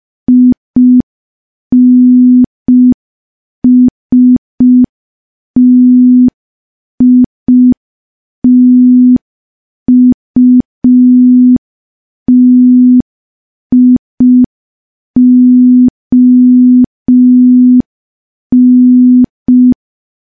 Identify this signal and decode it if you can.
Morse code, heard as a keyed tone, about 5 wpm, 250 Hz. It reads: INSTITUTION
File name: morse / morse-training-05WPM